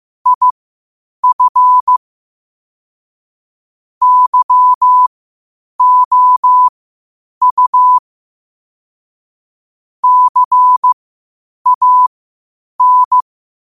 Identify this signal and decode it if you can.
Morse code, heard as a keyed tone, about 15 wpm, spaced as for 8 wpm, 1000 Hz. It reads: IF YOU CAN
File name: qotdfiles / 8wpm